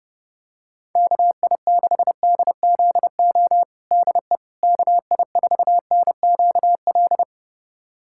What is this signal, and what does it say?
Morse code, heard as a keyed tone, about 30 wpm, 700 Hz. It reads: KI6DZO DE KI4NQL
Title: USA Callsign DE Callsign 30wpm